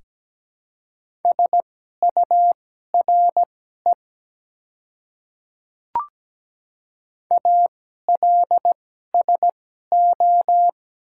Morse code, heard as a keyed tone, about 17 wpm, spaced as for 12 wpm, 700 Hz. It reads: SURE ALSO
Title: 2x Character Spacing - Top 300 Words - 17wpm